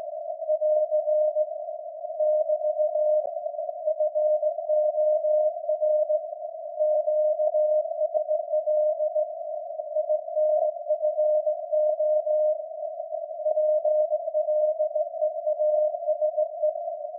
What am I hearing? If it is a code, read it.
Morse code, heard as a keyed tone, about 17 wpm, 610 Hz. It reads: <AR> <BT> FOR QSL INFO PLEASE
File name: iz7slz / VLF / SAQ jun 2019